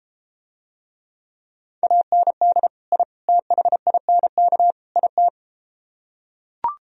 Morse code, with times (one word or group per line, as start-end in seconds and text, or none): 1.8-2.7 AND
2.9-3.0 I
3.3-4.7 THINK
5.0-5.3 IT